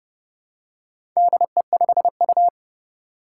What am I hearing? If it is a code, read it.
Morse code, heard as a keyed tone, about 30 wpm, 700 Hz. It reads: DE5U
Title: Four Challenging Letter-Number Combinations 30wpm